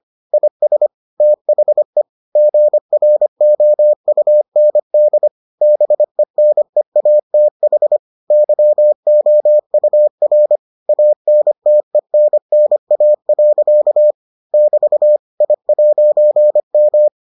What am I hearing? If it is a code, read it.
Morse code, heard as a keyed tone, about 25 wpm, 600 Hz. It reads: IS THE GROUND BENEATH YOUR ANTENNA. <BT> I'M